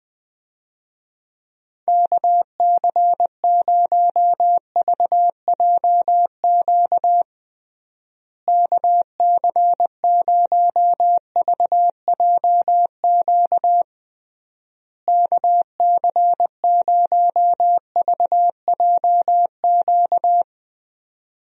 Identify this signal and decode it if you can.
Morse code, heard as a keyed tone, about 20 wpm, 700 Hz. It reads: KC0VJQ KC0VJQ KC0VJQ
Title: USA Call Signs - 6 Characters - Repeated 3x - 20wpm